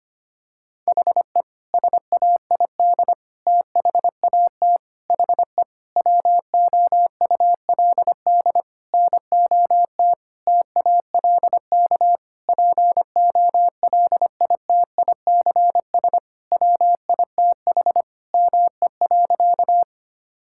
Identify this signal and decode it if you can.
Morse code, heard as a keyed tone, about 25 wpm, 700 Hz. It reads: HE SAID THAT HE WOULD NOT TALK POLITICS WITH ME.
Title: Sentences from Top 1000 Words - Review Part 6 - 25wpm